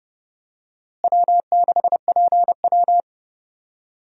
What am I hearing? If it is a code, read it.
Morse code, heard as a keyed tone, about 30 wpm, 700 Hz. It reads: W6PW
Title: USA Call Signs - 4 Characters - Repeated 2x - 30wpm